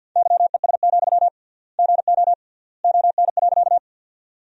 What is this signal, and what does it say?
Morse code, heard as a keyed tone, about 52 wpm, 700 Hz. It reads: YES, GO ON.
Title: 3x Word-Spacing - Sentences from Top 500 Words - 50wpm